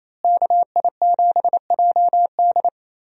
Morse code, heard as a keyed tone, about 28 wpm, 700 Hz. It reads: KI7JD
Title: USA Call Signs - 5 Characters - Repeated 2x - 28wpm